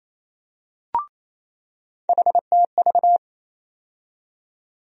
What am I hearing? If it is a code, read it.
Morse code, heard as a keyed tone, about 28 wpm, 700 Hz. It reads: HTV
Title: USA Call Sign Postfixes - 3 Letters 28wpm